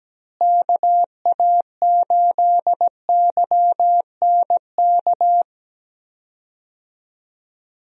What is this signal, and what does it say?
Morse code, heard as a keyed tone, about 17 wpm, 700 Hz. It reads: KA8YNK
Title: USA Call Signs - 6 Characters - Repeated 3x - 17wpm